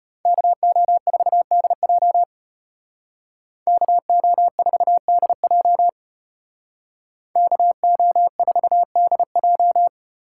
Morse code, heard as a keyed tone, about 34 wpm, 700 Hz. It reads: KO4DJ KO4DJ KO4DJ